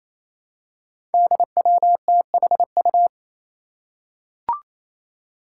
Morse code, heard as a keyed tone, about 28 wpm, 700 Hz. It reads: DWTHU
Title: Five Challenging Letter-Number Combinations 28wpm